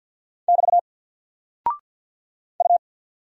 Morse code, heard as a keyed tone, about 48 wpm, 700 Hz. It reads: <BT> U
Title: Single Character (Letter:Number:Punctuation:Prosign) Alt 50wpm